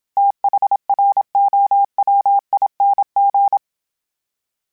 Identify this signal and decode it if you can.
Morse code, heard as a keyed tone, about 27 wpm, 800 Hz. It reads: THROWING
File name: THs / TH / TH Prefixes 3 @25